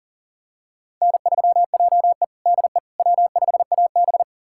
Morse code, heard as a keyed tone, about 40 wpm, 700 Hz. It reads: N3JE DE W5AB
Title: USA Callsign DE Callsign - 4 Characters - 40wpm